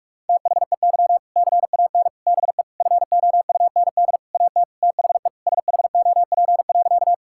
Morse code, heard as a keyed tone, about 46 wpm, 700 Hz. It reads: THEY CAN BE FOUND AT THE SHOP.
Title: Sentences from Top 1600 Words 45wpm